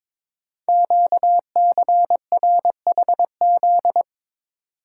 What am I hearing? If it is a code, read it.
Morse code, heard as a keyed tone, about 22 wpm, 700 Hz. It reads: QCRHZ
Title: Code Groups - 5 Letters - 22wpm